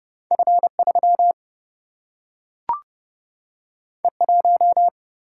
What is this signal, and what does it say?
Morse code, heard as a keyed tone, about 30 wpm, 700 Hz. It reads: F 3 E 1
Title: Single-Letter Single-Number 30wpm